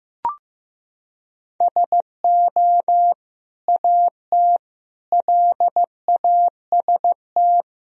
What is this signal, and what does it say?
Morse code, heard as a keyed tone, about 16 wpm, 700 Hz. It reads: SO AT LAST
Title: Sentences from Top 1000 Words - Review Part 1 - 15wpm